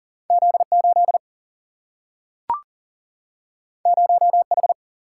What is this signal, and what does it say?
Morse code, heard as a keyed tone, about 41 wpm, 700 Hz. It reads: Z8 0H
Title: Single-Letter Single-Number 40wpm